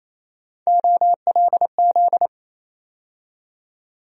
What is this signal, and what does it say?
Morse code, heard as a keyed tone, about 28 wpm, 700 Hz. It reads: OLZ
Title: USA Call Sign Postfixes - 3 Letters 28wpm